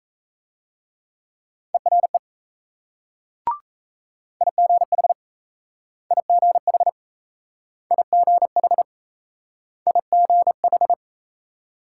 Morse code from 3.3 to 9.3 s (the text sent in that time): IGH IGH IGH